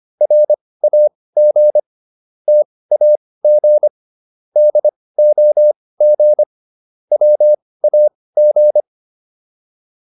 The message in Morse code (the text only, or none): RAG TAG DOG WAG